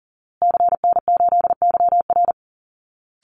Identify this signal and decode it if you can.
Morse code, heard as a keyed tone, about 40 wpm, 700 Hz. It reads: CN8YR
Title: Call Signs Repeated 2x - 40wpm